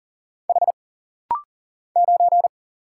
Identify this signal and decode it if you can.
Morse code, heard as a keyed tone, about 40 wpm, 700 Hz. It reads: H 9
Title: Single Letter-Number Alt - Rapid-Fire 40wpm